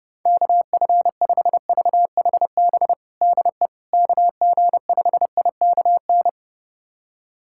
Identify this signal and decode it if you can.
Morse code, heard as a keyed tone, about 30 wpm, 700 Hz. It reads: KF5VHB DE KG5IKN
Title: USA Callsign DE Callsign - 6 Characters - 30wpm